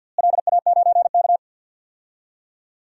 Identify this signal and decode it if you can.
Morse code, heard as a keyed tone, about 45 wpm, 700 Hz. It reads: LA9K
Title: DXCC-100 Call Signs - 4 Characters - Repeated 2x - 50wpm